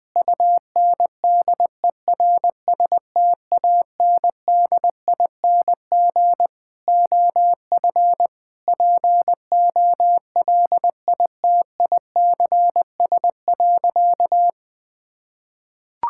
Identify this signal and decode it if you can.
Morse code, heard as a keyed tone, about 21 wpm, 700 Hz. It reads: UNDERSTANDING OF POLITICS.
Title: Sentences from Top 1000 Words - Review Part 6 - 20wpm